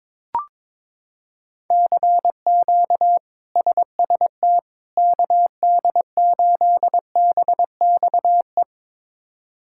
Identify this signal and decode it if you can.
Morse code, heard as a keyed tone, about 22 wpm, 700 Hz. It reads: CQ SST KD8BXE
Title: K1USN SST QSO Practice - 22wpm